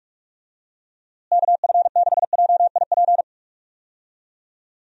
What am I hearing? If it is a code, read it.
Morse code, heard as a keyed tone, about 41 wpm, 700 Hz. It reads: KF6JIP
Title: USA Call Signs - 6 Characters - Repeated 3x - 45wpm